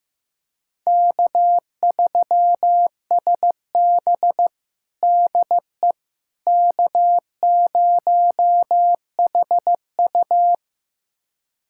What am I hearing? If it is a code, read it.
Morse code, heard as a keyed tone, about 16 wpm, 700 Hz. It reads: K3SB DE K0HU
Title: USA Callsign DE Callsign - 4 Characters - 15wpm